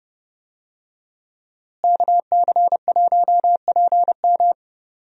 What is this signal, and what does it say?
Morse code, heard as a keyed tone, about 30 wpm, 700 Hz. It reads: KC1PM
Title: USA Call Signs - 5 Characters - Repeated 2x - 30wpm